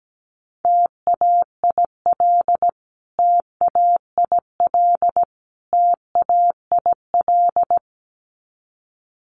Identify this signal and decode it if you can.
Morse code, heard as a keyed tone, about 17 wpm, 700 Hz. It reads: TAIL TAIL TAIL
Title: Top 500 Words Repeated 17wpm